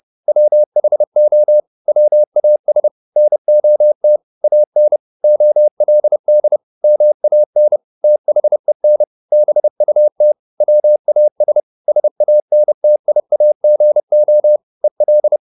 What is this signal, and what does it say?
Morse code, heard as a keyed tone, about 30 wpm, 600 Hz. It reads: WHO WAS NOT AN OLD MAN THEN BUT WAS SANTIAGO EL
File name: the old man and the sea 2 30wpm